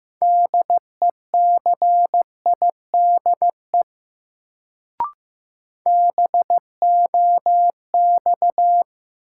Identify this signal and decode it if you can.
Morse code, heard as a keyed tone, about 15 wpm, 700 Hz. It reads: DECIDE BOX
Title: Sentences from Top 500 Words - New Word Drill - 15wpm